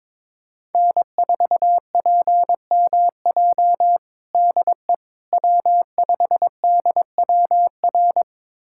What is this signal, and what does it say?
Morse code, heard as a keyed tone, about 22 wpm, 700 Hz. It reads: N4PMJ DE W5DWR
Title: USA Callsign DE Callsign - 5 Characters - 22wpm